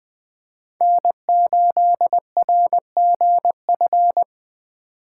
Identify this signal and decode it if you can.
Morse code, heard as a keyed tone, about 20 wpm, 700 Hz. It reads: N8RGF
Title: Speed Racing - USA 1x3 Call Signs 17wpm